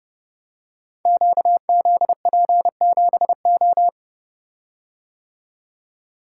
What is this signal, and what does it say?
Morse code, heard as a keyed tone, about 30 wpm, 700 Hz. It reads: QZP7O
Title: Code Groups - 5 Letter:Numbers - 30wpm